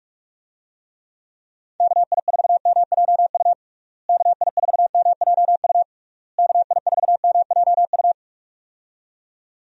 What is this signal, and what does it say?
Morse code, heard as a keyed tone, about 46 wpm, 700 Hz. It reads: KI4MJU KI4MJU KI4MJU
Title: USA Call Signs - 6 Characters - Repeated 3x - 45wpm